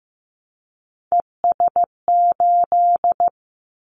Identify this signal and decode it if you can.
Morse code, heard as a keyed tone, about 15 wpm, 700 Hz. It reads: ES8
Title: Callsign Prefixes for Top 50 DXCC - 15wpm